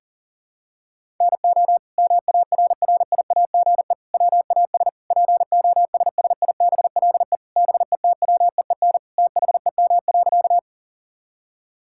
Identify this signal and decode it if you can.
Morse code, heard as a keyed tone, about 37 wpm, 700 Hz. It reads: NO MARRIAGE WAS POSSIBLE BETWEEN THEM.
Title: Sentences from Top 1000 Words - Review Part 5 - 40wpm